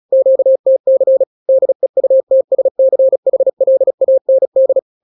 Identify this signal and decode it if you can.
Morse code, heard as a keyed tone, about 36 wpm, 530 Hz. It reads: QTC DEUTSCHLAND
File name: daily / 35wpm